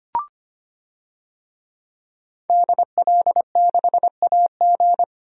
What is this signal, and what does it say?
Morse code, heard as a keyed tone, about 25 wpm, 700 Hz. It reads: DL6AG
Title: DXCC-100 Call Signs - 5 Characters - 25wpm